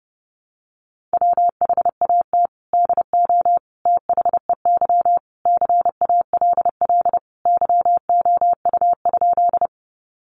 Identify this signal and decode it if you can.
Morse code, heard as a keyed tone, about 30 wpm, 700 Hz. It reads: WHAT DO THEY CALL YOU?
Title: Sentences from Top 100 Words 30wpm